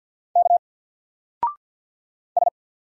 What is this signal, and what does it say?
Morse code, heard as a keyed tone, about 48 wpm, 700 Hz. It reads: K S
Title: Single Letter-Number Alt 50wpm